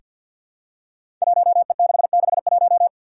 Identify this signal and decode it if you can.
Morse code, heard as a keyed tone, about 47 wpm, 700 Hz. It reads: 1E661
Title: Five Challenging Letter-Number Combinations 50wpm